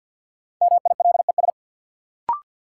Morse code, heard as a keyed tone, about 53 wpm, 700 Hz. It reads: MILES
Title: Sentences from Top 1000 Words - Review Part 3 - New Word Drill - 50wpm